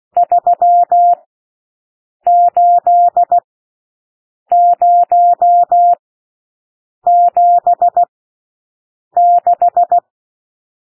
Morse code, heard as a keyed tone, about 16 wpm, spaced as for 6 wpm, 700 Hz. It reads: 38076
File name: SOUND NUMBERS 16 wpm